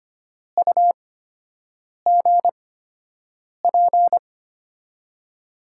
Morse code, heard as a keyed tone, about 25 wpm, spaced as for 7 wpm, 700 Hz. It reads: UGP
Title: USA Call Signs - 8x Character Spacing - 25wpm